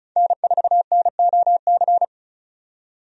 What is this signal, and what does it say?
Morse code, heard as a keyed tone, about 36 wpm, 700 Hz. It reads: N4NOC
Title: USA Call Signs - 5 Characters - 35wpm